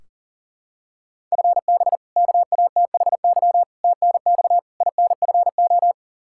Morse code, heard as a keyed tone, about 40 wpm, 700 Hz. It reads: FB KATHY TNX INFO
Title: Practice QSO - Exchange 3 - More Info - 40wpm